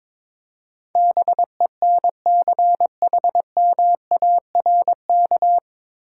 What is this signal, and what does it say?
Morse code, heard as a keyed tone, about 22 wpm, 690 Hz. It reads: BENCHMARK
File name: 9 Letter Words - First 3 Letters Given - 22wpm